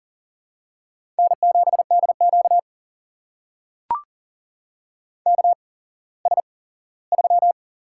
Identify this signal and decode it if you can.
Morse code, heard as a keyed tone, about 40 wpm, spaced as for 19 wpm, 700 Hz. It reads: N7DQ KS3
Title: USA Call Signs - 4 Characters - 8x Character Spacing - 40wpm